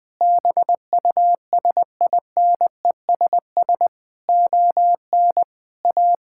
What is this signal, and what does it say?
Morse code, heard as a keyed tone, about 19 wpm, 700 Hz. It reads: BUSINESS ON A
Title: Sentences from Top 1000 Words - Review Part 5 - 20wpm